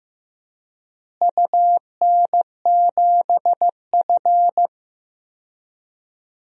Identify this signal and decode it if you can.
Morse code, heard as a keyed tone, about 15 wpm, 700 Hz. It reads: UN7F
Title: DXCC-100 Call Signs - 4 Characters - Repeated 2x - 15wpm